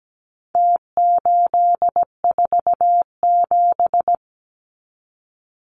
Morse code, heard as a keyed tone, about 17 wpm, 700 Hz. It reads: T847
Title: Numbers - Contest Style - 4-Digit 17wpm